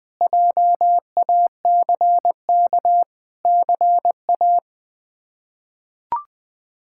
Morse code, heard as a keyed tone, about 20 wpm, 700 Hz. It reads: JACK CA
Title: K1USN SST QSO Practice - 20wpm